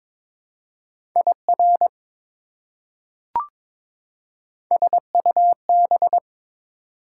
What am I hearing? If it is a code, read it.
Morse code, heard as a keyed tone, about 22 wpm, 700 Hz. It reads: IR SUB
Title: Prefixes and Suffixes 22wpm